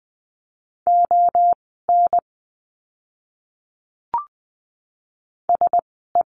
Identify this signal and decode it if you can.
Morse code, heard as a keyed tone, about 19 wpm, 700 Hz. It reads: O N S E
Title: 036 Lesson 7f - Letter s - Words 20wpm